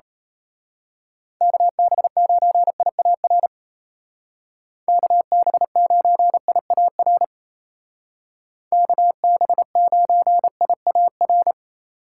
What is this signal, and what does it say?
Morse code, heard as a keyed tone, about 34 wpm, 700 Hz. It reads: KB9IAR KB9IAR KB9IAR
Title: Speed Racing - USA Call Signs 28wpm